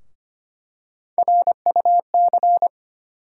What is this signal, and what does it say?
Morse code, heard as a keyed tone, about 25 wpm, 700 Hz. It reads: RUC
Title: USA Call Sign Postfixes - 3 Letters 25wpm